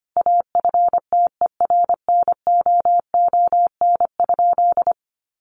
Morse code, heard as a keyed tone, about 25 wpm, 700 Hz. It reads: AFTERNOON?
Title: Sentences from Top 1500 Words 25wpm